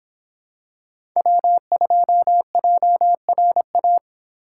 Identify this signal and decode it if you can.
Morse code, heard as a keyed tone, about 25 wpm, 700 Hz. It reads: W2JRA